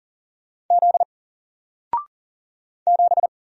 Morse code, heard as a keyed tone, about 40 wpm, 700 Hz. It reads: Z 7
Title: Single Letter-Number Alt 40wpm